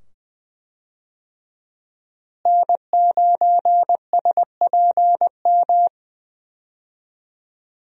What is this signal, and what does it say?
Morse code, heard as a keyed tone, about 20 wpm, 700 Hz. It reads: N9SPM
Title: USA Call Signs - 5 Characters - Repeated 2x - 20wpm